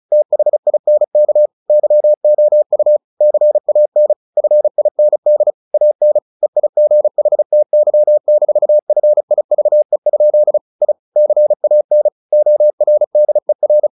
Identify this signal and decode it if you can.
Morse code, heard as a keyed tone, about 34 wpm, 600 Hz. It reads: THINK YOU CAN FIND AN EIGHTY-FIVE? I CAN ORDER
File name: the old man and the sea / the old man and the sea 1 35wpm